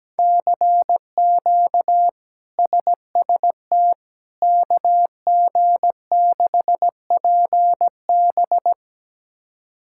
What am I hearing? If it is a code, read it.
Morse code, heard as a keyed tone, about 17 wpm, 700 Hz. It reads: CQ SST KG6PB
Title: K1USN SST QSO Practice - 17wpm